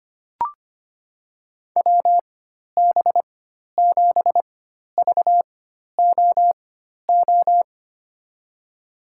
Morse code, heard as a keyed tone, about 25 wpm, spaced as for 12 wpm, 700 Hz. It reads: WB7VOO
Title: USA Call Signs - 6 Characters - 4x Character Spacing - 25wpm